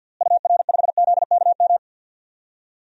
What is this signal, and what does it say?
Morse code, heard as a keyed tone, about 52 wpm, 700 Hz. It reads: UR5ZKM
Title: DXCC-100 Call Signs - 6 Characters - Repeated 3x - 50wpm